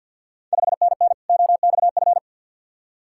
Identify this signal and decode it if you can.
Morse code, heard as a keyed tone, about 49 wpm, 690 Hz. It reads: 5NN OXF
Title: Ontario QSO Party Exchange 50wpm